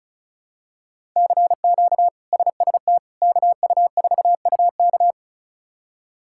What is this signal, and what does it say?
Morse code, heard as a keyed tone, about 34 wpm, 700 Hz. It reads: CQ SST KU4UK
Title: K1USN SST QSO Practice - 35wpm